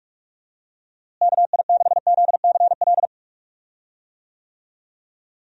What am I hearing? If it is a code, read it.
Morse code, heard as a keyed tone, about 43 wpm, 700 Hz. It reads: KI6ZCL
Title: USA Call Signs - 6 Characters - Repeated 2x - 45wpm